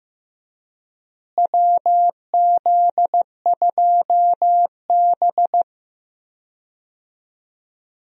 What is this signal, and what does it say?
Morse code, heard as a keyed tone, about 15 wpm, 700 Hz. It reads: WZ2B
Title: USA Call Signs - 4 Characters - Repeated 2x - 15wpm